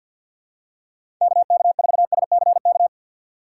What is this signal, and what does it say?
Morse code, heard as a keyed tone, about 48 wpm, 700 Hz. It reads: KK4SCK
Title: USA Call Signs - 3x Character Spacing - 50wpm